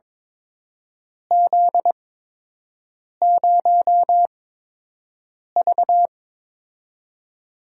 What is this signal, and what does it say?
Morse code, heard as a keyed tone, about 22 wpm, spaced as for 6 wpm, 700 Hz. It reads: Z0V
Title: USA Call Signs - 4 Characters - 8x Character Spacing - 22wpm